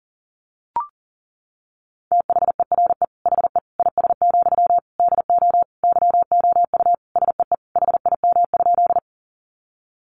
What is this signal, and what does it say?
Morse code, heard as a keyed tone, about 40 wpm, 700 Hz. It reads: THERE HE IS, DO YOU SEE HIM?
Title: Sentences from Top 100 Words 40wpm